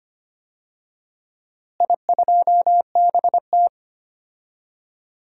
Morse code, heard as a keyed tone, about 25 wpm, 700 Hz. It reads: I2BT
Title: DXCC-100 Call Signs - 4 Characters - Repeated 3x - 25wpm